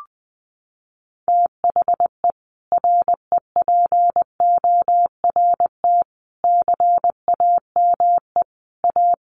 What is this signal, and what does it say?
Morse code, heard as a keyed tone, about 20 wpm, 700 Hz. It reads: THE REPORT CAME A
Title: Sentences from Top 900 Words 20wpm